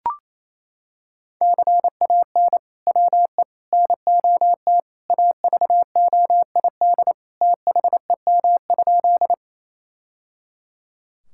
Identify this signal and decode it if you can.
Morse code, heard as a keyed tone, about 28 wpm, 700 Hz. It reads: CAN WE NOT AVOID THEM?
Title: Sentences from Top 1000 Words - Review Part 6 - 28wpm